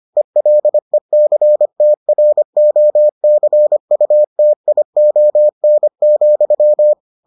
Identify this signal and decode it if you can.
Morse code, heard as a keyed tone, about 25 wpm, 600 Hz. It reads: ELECTROCUTION,